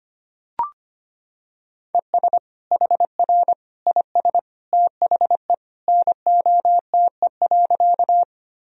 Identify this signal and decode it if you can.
Morse code, heard as a keyed tone, about 25 wpm, 700 Hz. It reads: ES HR IS THE NOTE.
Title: Sentences from Top 500 Words and Abbreviations 25wpm